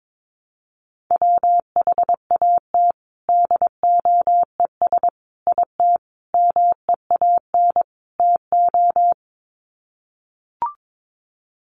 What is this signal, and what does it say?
Morse code, heard as a keyed tone, about 22 wpm, 700 Hz. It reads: WHAT DOES IT MEAN TO
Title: Sets of 5 Words Encore2 22wpm